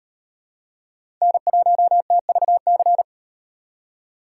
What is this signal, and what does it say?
Morse code, heard as a keyed tone, about 38 wpm, 700 Hz. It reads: N1TVC